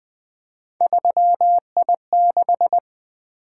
Morse code, heard as a keyed tone, about 20 wpm, 700 Hz. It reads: 3I6
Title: Three Letter-Number Combinations - 20wpm